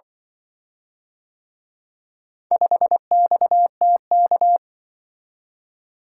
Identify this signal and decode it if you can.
Morse code, heard as a keyed tone, about 24 wpm, 700 Hz. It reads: 5XTK
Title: Speed-Racing - Four Letter-Number Combinations - 20wpm